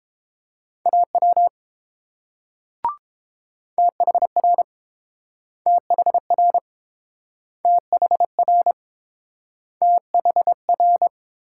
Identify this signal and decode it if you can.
Morse code, heard as a keyed tone, about 30 wpm, 700 Hz. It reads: AW THR THR THR THR